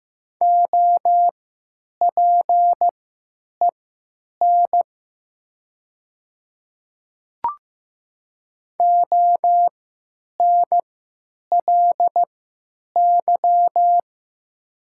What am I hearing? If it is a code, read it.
Morse code, heard as a keyed tone, about 15 wpm, spaced as for 8 wpm, 700 Hz. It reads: OPEN ONLY
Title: 3x Character Spacing - Top 300 Words - 15wpm